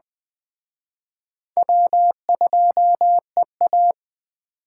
Morse code, heard as a keyed tone, about 21 wpm, 700 Hz. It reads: W2EA
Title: Speed Racing - USA 1x2 Call Signs 17wpm